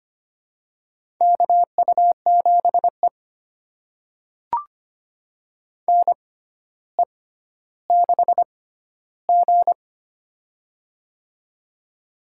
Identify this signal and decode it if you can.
Morse code, heard as a keyed tone, about 25 wpm, spaced as for 13 wpm, 700 Hz. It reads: KU7E NE6G
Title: USA Call Signs - 4 Characters - 6x Character Spacing - 25wpm